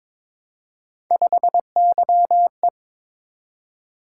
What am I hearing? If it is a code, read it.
Morse code, heard as a keyed tone, about 22 wpm, 700 Hz. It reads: 5YE
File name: practice / 22wpm / Three Letter-Number Combinations - 22wpm